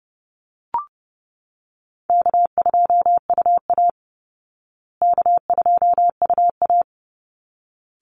Morse code, heard as a keyed tone, about 30 wpm, 700 Hz. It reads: K2UA K2UA
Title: Easy Call Signs Repeated 2x - 30wpm